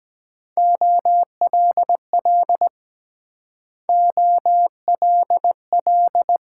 Morse code, heard as a keyed tone, about 22 wpm, 700 Hz. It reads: OLL OLL